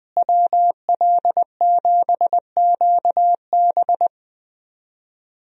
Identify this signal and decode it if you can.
Morse code, heard as a keyed tone, about 20 wpm, 700 Hz. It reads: WL7QB
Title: USA Call Signs - 5 Characters - Repeated 2x - 20wpm